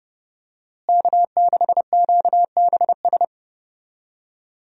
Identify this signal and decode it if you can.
Morse code, heard as a keyed tone, about 30 wpm, 700 Hz. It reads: K6QBS
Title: USA Call Signs - 5 Characters - Repeated 2x - 30wpm